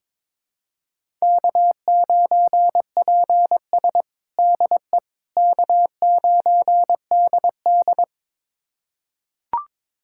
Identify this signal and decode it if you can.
Morse code, heard as a keyed tone, about 21 wpm, 700 Hz. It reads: K9PS DE K9DD
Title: USA Callsign DE Callsign - 4 Characters - 22wpm